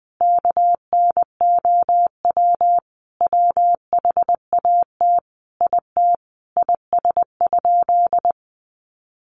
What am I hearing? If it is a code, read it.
Morse code, heard as a keyed tone, about 20 wpm, 700 Hz. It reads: KNOW WHAT IT IS?
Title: Sentences from Top 100 Words 20wpm